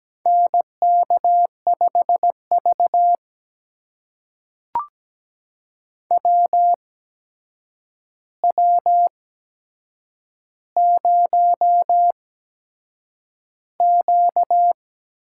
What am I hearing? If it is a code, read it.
Morse code, heard as a keyed tone, about 17 wpm, spaced as for 7 wpm, 700 Hz. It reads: NK5V WW0Q
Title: USA Call Signs - 8x Character Spacing - 17wpm